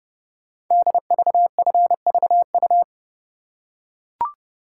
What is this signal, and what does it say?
Morse code, heard as a keyed tone, about 30 wpm, 700 Hz. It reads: DVFVU
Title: Code Groups - 5 Letters - 30wpm